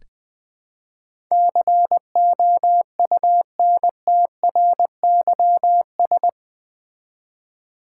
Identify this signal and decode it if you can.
Morse code, heard as a keyed tone, about 20 wpm, 700 Hz. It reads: COUNTRYS
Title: Build A Word - 4 Letters At A Time - 11-Letter Words - 20wpm